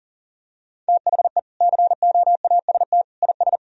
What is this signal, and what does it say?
Morse code, heard as a keyed tone, about 40 wpm, 700 Hz. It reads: THE COAST IS
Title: Sentences from Top 1000 Words - Review Part 4 - 40wpm